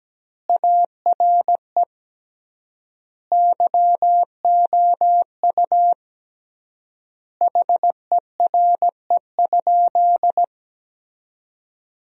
Morse code, heard as a keyed tone, about 17 wpm, 700 Hz. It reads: ARE YOU HERE?
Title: 3x Word-Spacing - Sentences from Top 200 Words - 17wpm